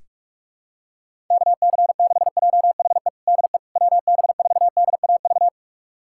Morse code, heard as a keyed tone, about 46 wpm, 700 Hz. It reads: KC6JHE DE WB4DAV
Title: USA Callsign DE Callsign 45wpm